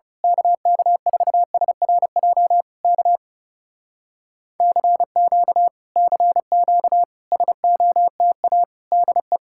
Morse code, heard as a keyed tone, about 28 wpm, 700 Hz. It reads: KK4SRJ K CQ CQ SOTA DE
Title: Speed-Racing Common QSO Phrases Encore - 30wpm